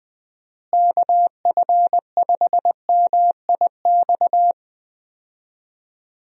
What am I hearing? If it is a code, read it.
Morse code, heard as a keyed tone, about 20 wpm, 700 Hz. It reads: KF5MIX
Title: USA Call Signs - 6 Characters - Repeated 3x - 20wpm